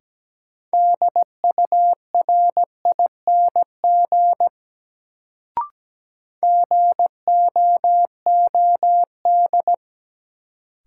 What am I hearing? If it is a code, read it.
Morse code, heard as a keyed tone, about 17 wpm, 700 Hz. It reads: DURING GOOD
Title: Sets of 2 Words - New Word Drill - 17wpm